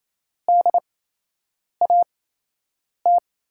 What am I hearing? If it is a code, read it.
Morse code, heard as a keyed tone, about 27 wpm, spaced as for 8 wpm, 700 Hz. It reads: DAT
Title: USA Call Signs - 6 Characters - 8x Character Spacing - 28wpm